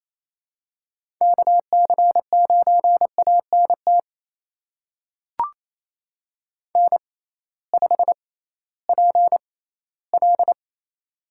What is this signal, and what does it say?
Morse code, heard as a keyed tone, about 28 wpm, spaced as for 16 wpm, 700 Hz. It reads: KC9ANT N5PL
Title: USA Call Signs - 6x Character Spacing - 28wpm